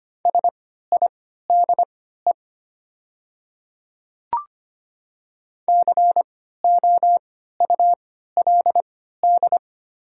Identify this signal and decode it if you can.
Morse code, heard as a keyed tone, about 25 wpm, spaced as for 14 wpm, 690 Hz. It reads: SIDE COULD
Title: 3x Character Spacing - Top 200 Words - 25wpm